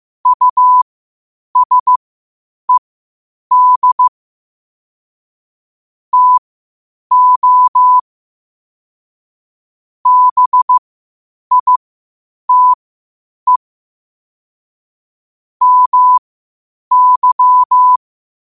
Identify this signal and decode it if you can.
Morse code, heard as a keyed tone, about 15 wpm, spaced as for 8 wpm, 1000 Hz. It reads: USED TO BITE MY
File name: qotdfiles / 8wpm